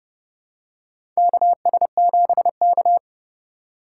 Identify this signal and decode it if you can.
Morse code, heard as a keyed tone, about 30 wpm, 700 Hz. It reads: KS7K
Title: USA Call Signs - 4 Characters - 3x Character Spacing - 30wpm